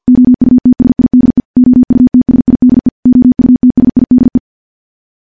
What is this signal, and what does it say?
Morse code, heard as a keyed tone, about 51 wpm, 250 Hz. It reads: OUTSIDE OUTSIDE OUTSIDE
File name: morse / morse-training-50WPM